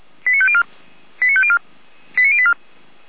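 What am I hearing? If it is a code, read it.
Five-tone selective call (ZVEI), heard as quick runs of five tones, tones of about 70 ms. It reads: 89504 89504 89084
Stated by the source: ZVEI five-tone selective call received by SDR.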